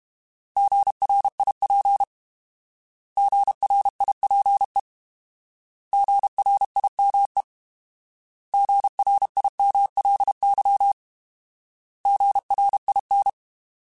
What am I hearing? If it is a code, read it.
Morse code, heard as a keyed tone, about 32 wpm, 800 Hz. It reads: GRIP GRIPE GRIME GRIMLY GRIN
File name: Alphabetical / Alphabetical mp3s / Gs @30